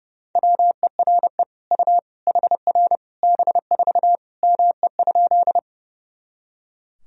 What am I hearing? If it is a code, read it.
Morse code, heard as a keyed tone, about 30 wpm, 700 Hz. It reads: WERE U HR B4 ME?
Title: Sentences from Top 200 Words and Abbreviations 30wpm